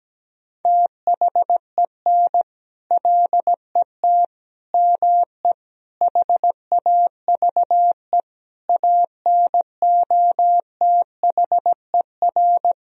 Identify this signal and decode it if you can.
Morse code, heard as a keyed tone, about 17 wpm, 700 Hz. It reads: THEN LET ME HAVE ANOTHER
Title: Sentences from Top 1000 Words - Review Part 2 - 17wpm